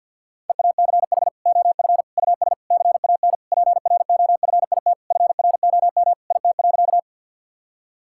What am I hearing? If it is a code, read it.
Morse code, heard as a keyed tone, about 52 wpm, 700 Hz. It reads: EACH OF US CAN PROFIT FROM IT.
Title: Sentences from Top 1000 Words - Review Part 5 - 50wpm